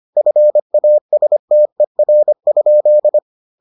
Morse code, heard as a keyed tone, about 25 wpm, 600 Hz. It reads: FASTER?